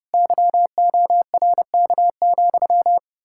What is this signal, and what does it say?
Morse code, heard as a keyed tone, about 38 wpm, 700 Hz. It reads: YORK,